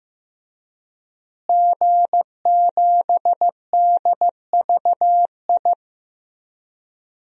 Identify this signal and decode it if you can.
Morse code, heard as a keyed tone, about 15 wpm, 700 Hz. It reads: G7DVI
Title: DXCC-100 Call Signs - 5 Characters - Repeated 3x - 15wpm